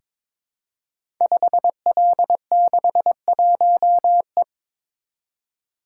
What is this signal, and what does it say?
Morse code, heard as a keyed tone, about 22 wpm, 700 Hz. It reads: 5L61E
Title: Code Groups - 5 Letter:Numbers - 22wpm